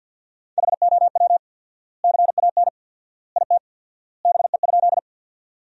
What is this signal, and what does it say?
Morse code, heard as a keyed tone, about 50 wpm, 700 Hz. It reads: HOW CAN IT BE?
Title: 4x Word-Spacing - Sentences from Top 100 Words - 50wpm